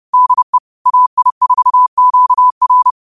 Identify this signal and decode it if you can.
Morse code, heard as a keyed tone, about 30 wpm, 1000 Hz. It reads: DE AI4QR